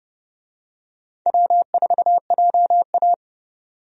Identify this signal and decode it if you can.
Morse code, heard as a keyed tone, about 30 wpm, 700 Hz. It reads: W4JA
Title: USA Call Signs - 4 Characters - 30wpm